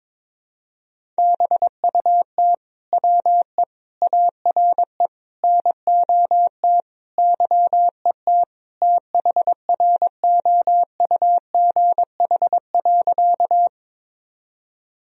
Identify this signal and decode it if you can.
Morse code, heard as a keyed tone, about 22 wpm, 700 Hz. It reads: BUT WE ARE NOT YET THROUGH.
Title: Sentences from Top 1000 Words - Review Part 1 - 22wpm